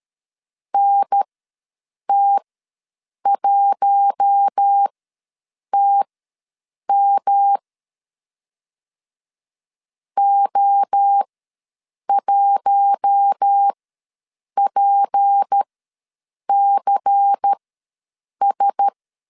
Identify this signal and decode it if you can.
Morse code, heard as a keyed tone, about 13 wpm, spaced as for 7 wpm, 800 Hz. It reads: NT1TM O1PCS
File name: LEZIONI / 6 PKX0 / 6 P ws 12wpm 3117